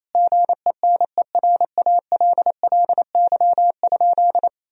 Morse code, heard as a keyed tone, about 28 wpm, 700 Hz. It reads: GENERALLY?
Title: Sentences from Top 1000 Words - Review Part 4 - 28wpm